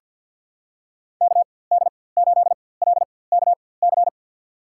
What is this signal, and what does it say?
Morse code, heard as a keyed tone, about 46 wpm, spaced as for 23 wpm, 700 Hz. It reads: KD8LKC
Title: USA Call Signs - 4x Character Spacing - 50wpm